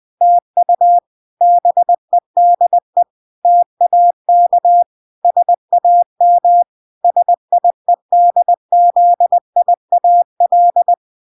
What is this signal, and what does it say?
Morse code, heard as a keyed tone, about 20 wpm, 700 Hz. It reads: TU BEDE TAK SAM SIEDZIAL